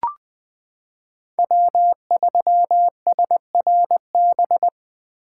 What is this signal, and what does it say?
Morse code, heard as a keyed tone, about 21 wpm, 700 Hz. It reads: W3SRB
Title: 2023 POTA Activator Call Signs - 20wpm